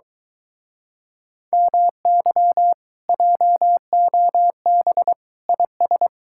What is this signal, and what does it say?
Morse code, heard as a keyed tone, about 23 wpm, 700 Hz. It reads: MY JOB IS